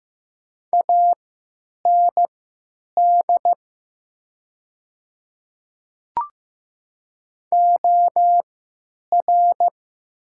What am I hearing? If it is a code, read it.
Morse code, heard as a keyed tone, about 15 wpm, spaced as for 9 wpm, 700 Hz. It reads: AND OR
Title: 3x Character Spacing - Top 100 Words - 15wpm